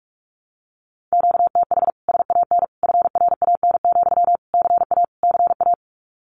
Morse code, heard as a keyed tone, about 44 wpm, 700 Hz. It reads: QTH SAN FRAN, CA CA
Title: Common QSO Phrases Encore 45wpm